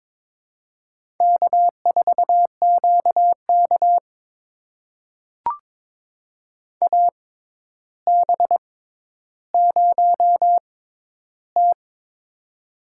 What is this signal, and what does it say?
Morse code, heard as a keyed tone, about 22 wpm, spaced as for 11 wpm, 700 Hz. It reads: K4QK AB0T
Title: USA Call Signs - 4 Characters - 6x Character Spacing - 22wpm